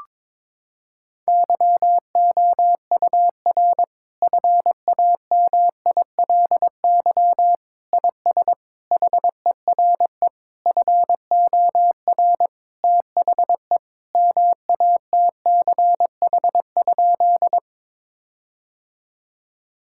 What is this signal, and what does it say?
Morse code, heard as a keyed tone, about 21 wpm, 700 Hz. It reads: YOUR FAMILY IS HERE FOR THE MATCH?
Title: Sentences from Top 1600 Words 22wpm